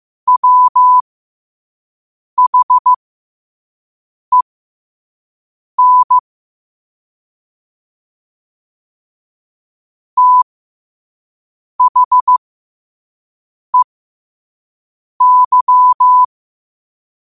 Morse code, heard as a keyed tone, about 15 wpm, spaced as for 5 wpm, 1000 Hz. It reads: WHEN THEY
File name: qotdfiles / 5wpm